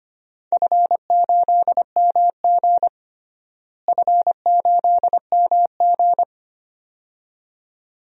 Morse code, heard as a keyed tone, about 25 wpm, 700 Hz. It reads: F8MG F8MG
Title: DXCC-100 Call Signs - 4 Characters - Repeated 2x - 25wpm